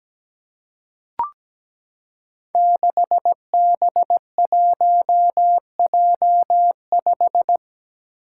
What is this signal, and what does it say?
Morse code, heard as a keyed tone, about 17 wpm, 700 Hz. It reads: 6B1J5
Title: Five Challenging Letter-Number Combinations 17wpm